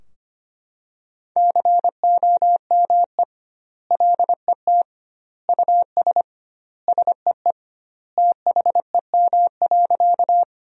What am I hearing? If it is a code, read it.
Morse code, heard as a keyed tone, about 25 wpm, 700 Hz. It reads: COME LET US SEE THEM.
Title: 2x Word-Spacing - Sentences from Top 300 Words - 25wpm